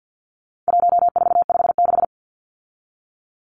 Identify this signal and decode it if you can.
Morse code, heard as a keyed tone, about 48 wpm, 700 Hz. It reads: 1456
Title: Numbers - 4-Digit 50wpm